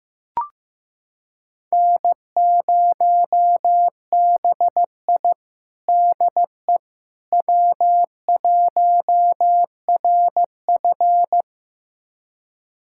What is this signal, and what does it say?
Morse code, heard as a keyed tone, about 16 wpm, 700 Hz. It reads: N0BI DE W1RF
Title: USA Callsign DE Callsign - 4 Characters - 15wpm